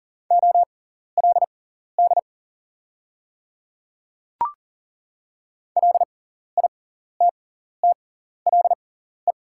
Morse code, heard as a keyed tone, about 41 wpm, spaced as for 14 wpm, 690 Hz. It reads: OLD LITTLE
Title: 6x Character Spacing - Top 200 Words - 40wpm